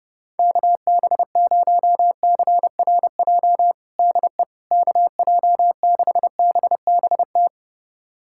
Morse code, heard as a keyed tone, about 30 wpm, 700 Hz. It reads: KB0CRJ DE KJ6BBT
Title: USA Callsign DE Callsign - 6 Characters - 30wpm